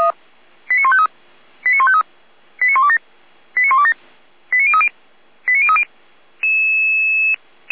Five-tone selective call (ZVEI), heard as quick runs of five tones, tones of about 70 ms. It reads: 89153 89153 89127 89127 89030 89030
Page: /455/SDR/ZVEI/